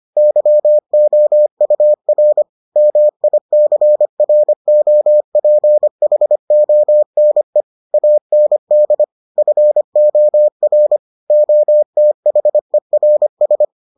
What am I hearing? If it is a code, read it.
Morse code, heard as a keyed tone, about 25 wpm, 600 Hz. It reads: YOUR MICROPHONE AND FOR OTHERS